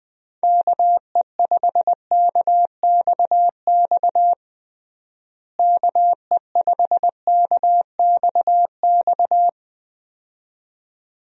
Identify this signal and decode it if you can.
Morse code, heard as a keyed tone, about 21 wpm, 700 Hz. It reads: KE5KXX KE5KXX
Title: USA Call Signs - 6 Characters - Repeated 2x - 20wpm